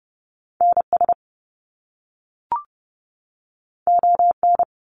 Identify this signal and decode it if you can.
Morse code, heard as a keyed tone, about 30 wpm, 700 Hz. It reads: NS ON
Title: Canadian Provinces and Territories 30wpm